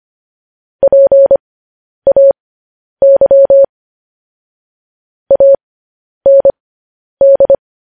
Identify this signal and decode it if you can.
Morse code, heard as a keyed tone, about 25 wpm, spaced as for 10 wpm, 550 Hz. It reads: PAY AND